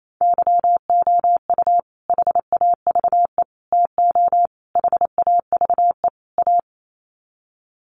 Morse code, heard as a keyed tone, about 28 wpm, 700 Hz. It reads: YOU HAVE TO HAVE A
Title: Sets of 5 Words Encore 28wpm